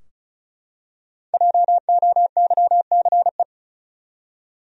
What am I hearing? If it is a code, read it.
Morse code, heard as a keyed tone, about 36 wpm, 700 Hz. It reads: JOYCE
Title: USA Common Names - Max 5 Letters - 35wpm